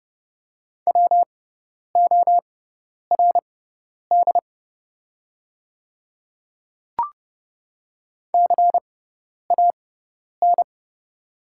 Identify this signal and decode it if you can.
Morse code, heard as a keyed tone, about 30 wpm, spaced as for 10 wpm, 700 Hz. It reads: WORD CAN
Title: 6x Character Spacing - Top 300 Words - 30wpm